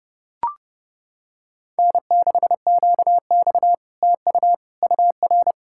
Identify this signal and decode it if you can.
Morse code, heard as a keyed tone, about 30 wpm, 700 Hz. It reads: N6QX TU UR